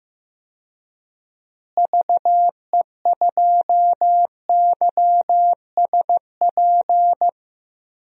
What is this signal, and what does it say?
Morse code, heard as a keyed tone, about 15 wpm, 700 Hz. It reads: VE2YSP
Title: DXCC-100 Call Signs - 6 Characters - Repeated 2x - 15wpm